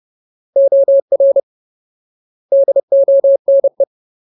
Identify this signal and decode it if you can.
Morse code, heard as a keyed tone, about 30 wpm, 550 Hz. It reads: OR DONE
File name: Ultimate Word List / 30WPM Ultimate